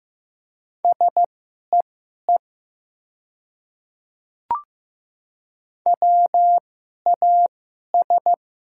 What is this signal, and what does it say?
Morse code, heard as a keyed tone, about 15 wpm, spaced as for 11 wpm, 700 Hz. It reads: SEE WAS
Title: 2x Character Spacing - Top 100 Words - 15wpm